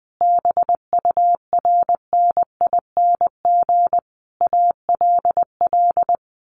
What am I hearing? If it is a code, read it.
Morse code, heard as a keyed tone, about 20 wpm, 700 Hz. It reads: BURNING ALL
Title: A Sound Of Thunder 20wpm